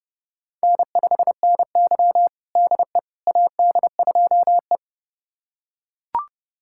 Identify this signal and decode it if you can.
Morse code, heard as a keyed tone, about 30 wpm, 700 Hz. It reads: N5NY DE AD2E
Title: USA Callsign DE Callsign - 4 Characters - 30wpm